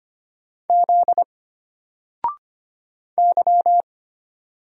Morse code, heard as a keyed tone, about 25 wpm, 700 Hz. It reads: Z Y
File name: Single-Letter-Number-25wpm